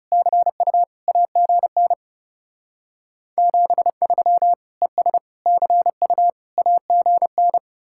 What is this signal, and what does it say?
Morse code, heard as a keyed tone, about 35 wpm, 700 Hz. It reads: CU AGN 73 ES CU AGN
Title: Speed-Racing Common QSO Phrases Encore - 30wpm